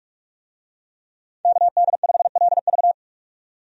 Morse code, heard as a keyed tone, about 46 wpm, 700 Hz. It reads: KD5LV
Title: USA Call Signs - 5 Characters - 45wpm